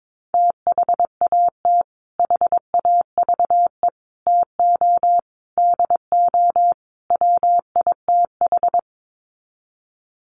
Morse code, heard as a keyed tone, about 22 wpm, 700 Hz. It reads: THAT HAVE TO DO WITH
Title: Sets of 5 Words Encore2 22wpm